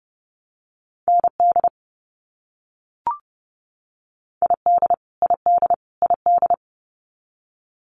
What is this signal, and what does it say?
Morse code, heard as a keyed tone, about 30 wpm, 700 Hz. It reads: ND ID ID ID
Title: US States Repeated 3x 30wpm